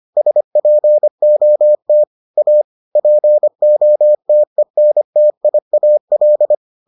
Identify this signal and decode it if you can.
Morse code, heard as a keyed tone, about 25 wpm, 600 Hz. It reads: SPOT A POTENTIAL